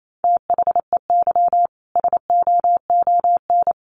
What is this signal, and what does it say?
Morse code, heard as a keyed tone, about 28 wpm, 700 Hz. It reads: THEY SOON
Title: Sentences from Top 1200 Words 28wpm